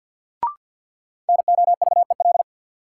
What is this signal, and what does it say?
Morse code, heard as a keyed tone, about 49 wpm, 700 Hz. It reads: NOVEL
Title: Sentences from Top 1300 Words - New Word Drill - 50wpm